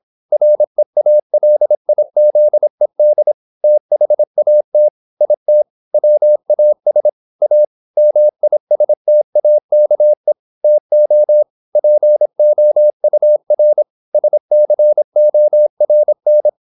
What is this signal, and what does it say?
Morse code, heard as a keyed tone, about 26 wpm, 600 Hz. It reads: REALIZED THAT IT WAS A MISTAKE TO POUR SCORN